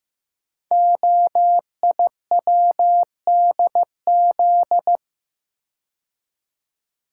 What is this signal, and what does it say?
Morse code, heard as a keyed tone, about 15 wpm, 700 Hz. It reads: OIWDZ
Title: Code Groups - 5 Letters - 15wpm